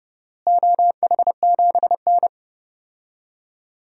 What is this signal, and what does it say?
Morse code, heard as a keyed tone, about 30 wpm, 700 Hz. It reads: OH7N
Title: DXCC-100 Call Signs - 4 Characters - Repeated 3x - 30wpm